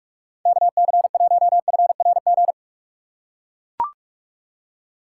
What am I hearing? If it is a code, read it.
Morse code, heard as a keyed tone, about 42 wpm, 700 Hz. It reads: KC1FRG
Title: USA Call Signs - 6 Characters - 6x Character Spacing - 45wpm